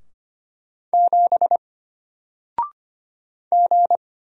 Morse code, heard as a keyed tone, about 25 wpm, 700 Hz. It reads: 7 G
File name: Single-Letter-Number-25wpm